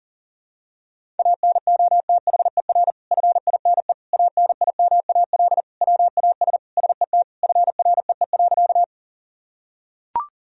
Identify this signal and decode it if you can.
Morse code, heard as a keyed tone, about 40 wpm, 700 Hz. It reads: ANOTHER FINE ANIMAL WAS SET FREE.
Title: Sentences from Top 1000 Words - Review Part 1 - 40wpm